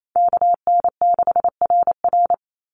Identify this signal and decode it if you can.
Morse code, heard as a keyed tone, about 28 wpm, 700 Hz. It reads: KN6RR
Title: USA Call Signs 28wpm